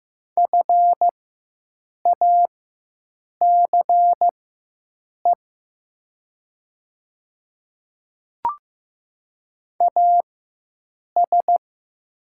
Morse code, heard as a keyed tone, about 15 wpm, spaced as for 7 wpm, 700 Hz. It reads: FACE AS
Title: 4x Character Spacing - Top 300 Words - 15wpm